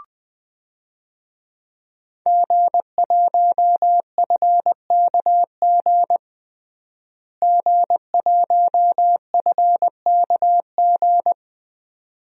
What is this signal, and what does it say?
Morse code, heard as a keyed tone, about 20 wpm, 700 Hz. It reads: G1FKG G1FKG
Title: DXCC-100 Call Signs - 5 Characters - Repeated 2x - 20wpm